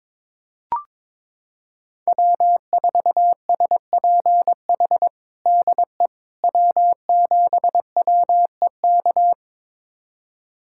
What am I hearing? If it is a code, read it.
Morse code, heard as a keyed tone, about 22 wpm, 700 Hz. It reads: W4SPH DE W7WEK
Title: USA Callsign DE Callsign - 5 Characters - 22wpm